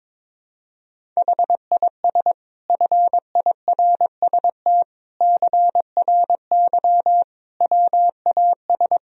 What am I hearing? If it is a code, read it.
Morse code, heard as a keyed tone, about 22 wpm, 700 Hz. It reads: HIS FIRST CRY WAS
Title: Sentences from Top 500 Words and Abbreviations 22wpm